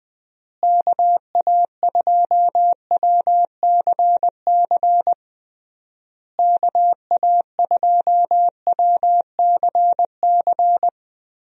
Morse code, heard as a keyed tone, about 20 wpm, 700 Hz. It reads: KA2WCC KA2WCC
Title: USA Call Signs - 6 Characters - Repeated 3x - 20wpm